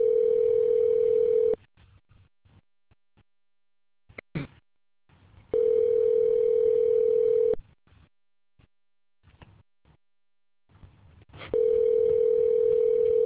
Ringback tone (North America), 440 + 480 Hz.